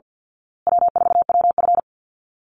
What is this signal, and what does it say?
Morse code, heard as a keyed tone, about 50 wpm, 700 Hz. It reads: W4WF
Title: Easy Call Signs Repeated 3x - 50wpm